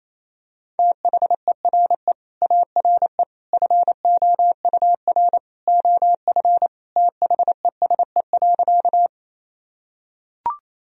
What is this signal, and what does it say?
Morse code, heard as a keyed tone, about 28 wpm, 700 Hz. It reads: THERE ARE FOUR OF THESE.
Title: Sentences from Top 1000 Words - Review Part 1 - 28wpm